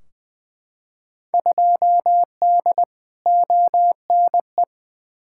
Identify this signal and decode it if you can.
Morse code, heard as a keyed tone, about 20 wpm, 700 Hz. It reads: 2D ONE
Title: Field Day Exchanges - 20wpm